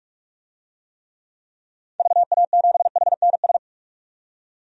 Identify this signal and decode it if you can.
Morse code, heard as a keyed tone, about 46 wpm, 700 Hz. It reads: VA7HNS
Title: DXCC-100 Call Signs - 6 Characters - 45wpm